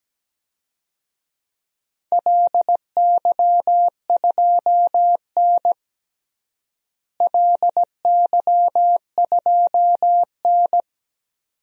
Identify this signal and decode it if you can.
Morse code, heard as a keyed tone, about 17 wpm, 700 Hz. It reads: LY2N LY2N
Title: DXCC-100 Call Signs - 4 Characters - Repeated 2x - 17wpm